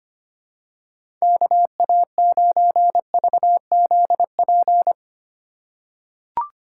Morse code, heard as a keyed tone, about 25 wpm, 700 Hz. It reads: KA9VZP
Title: USA Call Signs - 6 Characters - 3x Character Spacing - 25wpm